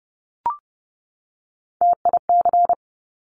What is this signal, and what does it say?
Morse code, heard as a keyed tone, about 30 wpm, 700 Hz. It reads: TIC
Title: Top Three Letter Combinations in Words 30wpm